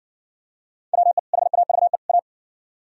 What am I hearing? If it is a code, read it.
Morse code, heard as a keyed tone, about 58 wpm, 700 Hz. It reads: WE HAVE A